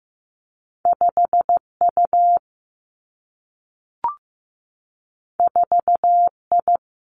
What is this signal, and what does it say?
Morse code, heard as a keyed tone, about 15 wpm, 700 Hz. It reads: 5 U 4 I
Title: Two Challenging Letter-Number Combinations 15wpm